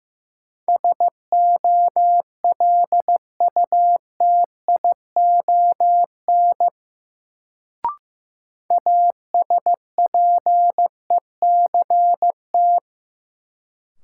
Morse code, heard as a keyed tone, about 15 wpm, 700 Hz. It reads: SOLUTION ASPECT
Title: Sentences from Top 1300 Words - New Word Drill Alt - 15wpm